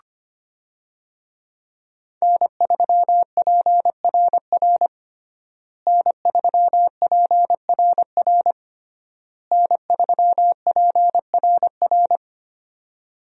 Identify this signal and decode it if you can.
Morse code, heard as a keyed tone, about 25 wpm, 700 Hz. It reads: N3PRR N3PRR N3PRR
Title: USA Call Signs - 5 Characters - Repeated 3x - 25wpm